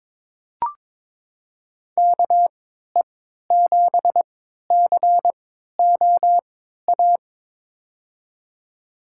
Morse code, heard as a keyed tone, about 21 wpm, spaced as for 12 wpm, 700 Hz. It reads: KE7COA
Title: USA Call Signs - 6 Characters - 3x Character Spacing - 22wpm